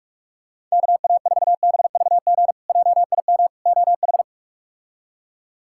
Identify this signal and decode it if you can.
Morse code, heard as a keyed tone, about 41 wpm, 700 Hz. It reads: KA4BVG JIM OH
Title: K1USN SST QSO Practice - 45wpm